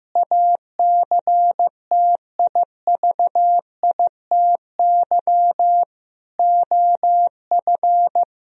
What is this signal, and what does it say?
Morse code, heard as a keyed tone, about 15 wpm, 700 Hz. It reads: ACTIVITY OF
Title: Sentences from Top 1000 Words - Review Part 4 - 15wpm